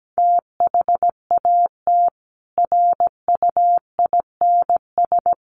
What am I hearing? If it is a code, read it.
Morse code, heard as a keyed tone, about 17 wpm, 700 Hz. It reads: THAT RUINS
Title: A Sound Of Thunder 17wpm